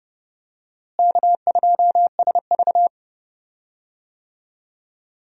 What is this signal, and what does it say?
Morse code, heard as a keyed tone, about 30 wpm, 700 Hz. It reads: K2SV
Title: USA Call Signs - 4 Characters - Repeated 3x - 30wpm